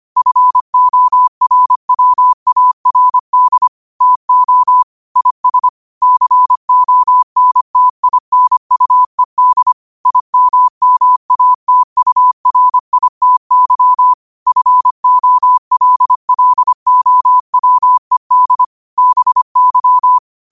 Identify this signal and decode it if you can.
Morse code, heard as a keyed tone, about 25 wpm, 1000 Hz. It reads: FORWARD TO IS CONTINUED IMMATURITY FOLLOWED BY
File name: qotdfiles / 25wpm